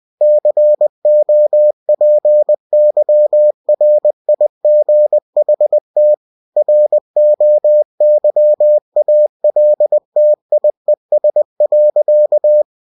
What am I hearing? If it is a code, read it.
Morse code, heard as a keyed tone, about 20 wpm, 600 Hz. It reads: COPYRIGHT ROYALTIES.
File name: cw-books / winnie-the-pooh / winnie-20wpm-page-0112